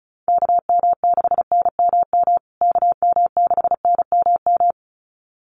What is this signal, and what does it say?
Morse code, heard as a keyed tone, about 34 wpm, 700 Hz. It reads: KM6NMM KM6NMM
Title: USA Call Signs Repeated 2x 35wpm